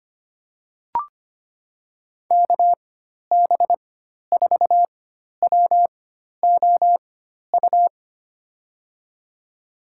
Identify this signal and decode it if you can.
Morse code, heard as a keyed tone, about 25 wpm, spaced as for 12 wpm, 700 Hz. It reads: KB4WOU
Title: USA Call Signs - 4x Character Spacing - 25wpm